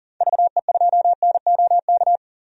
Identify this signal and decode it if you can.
Morse code, heard as a keyed tone, about 38 wpm, 700 Hz. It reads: VE2NOK
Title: DXCC-100 Call Signs - 6 Characters - 40wpm